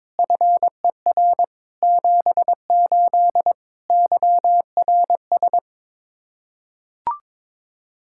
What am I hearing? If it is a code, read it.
Morse code, heard as a keyed tone, about 22 wpm, 700 Hz. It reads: FER 78 YRS
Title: Practice QSO - Exchange 3 - More Info - 22wpm